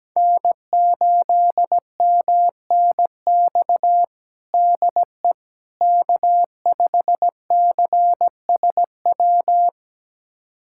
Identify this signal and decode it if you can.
Morse code, heard as a keyed tone, about 17 wpm, 700 Hz. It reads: N8MNX DE K5CSW
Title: USA Callsign DE Callsign 17wpm